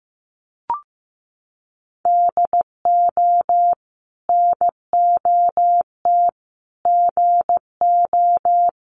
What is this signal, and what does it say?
Morse code, heard as a keyed tone, about 15 wpm, 700 Hz. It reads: DO NOT GO
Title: Sets of 4 Words 15wpm